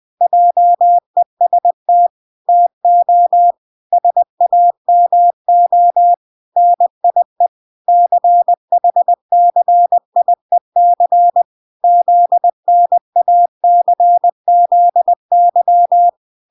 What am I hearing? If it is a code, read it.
Morse code, heard as a keyed tone, about 20 wpm, 700 Hz. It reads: JEST TO SAMO NIE CHCIEC ZNACZY